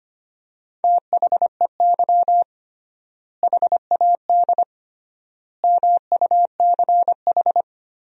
Morse code, heard as a keyed tone, about 25 wpm, 700 Hz. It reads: THEY HAD MUCH
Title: 3x Word-Spacing - Sentences from Top 400 Words - 25wpm